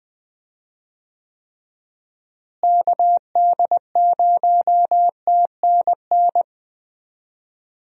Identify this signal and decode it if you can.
Morse code, heard as a keyed tone, about 20 wpm, 700 Hz. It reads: KD0TNN
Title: USA Call Signs - 6 Characters - Repeated 3x - 20wpm